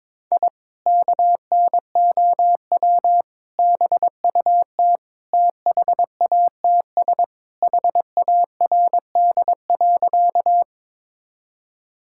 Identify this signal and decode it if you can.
Morse code, heard as a keyed tone, about 22 wpm, 700 Hz. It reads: I KNOW BUT THATS HARD.
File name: practice / 22wpm / Sentences with 2-Word Phrases - 22wpm